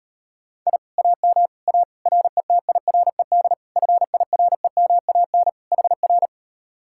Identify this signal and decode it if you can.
Morse code, heard as a keyed tone, about 38 wpm, 700 Hz. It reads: I AM A RETIRED FIREMAN HR